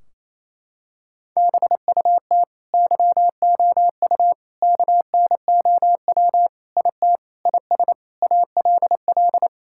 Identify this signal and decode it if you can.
Morse code, heard as a keyed tone, about 28 wpm, 700 Hz. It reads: BUT YOU KNOW IT IS ALL
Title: Sentences from Top 1600 Words 28wpm